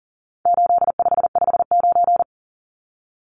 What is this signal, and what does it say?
Morse code, heard as a keyed tone, about 40 wpm, 700 Hz. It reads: 8559
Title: Numbers - 4-Digit 40wpm